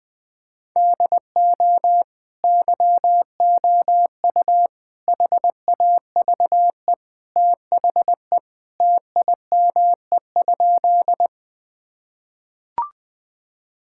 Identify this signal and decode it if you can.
Morse code, heard as a keyed tone, about 20 wpm, 700 Hz. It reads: DO YOU HAVE THE TIME?
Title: Sentences with 2-Word Phrases - 20wpm